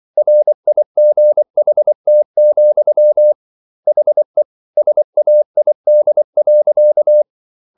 Morse code, heard as a keyed tone, about 24 wpm, 600 Hz. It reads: RIGHT, HE SAID.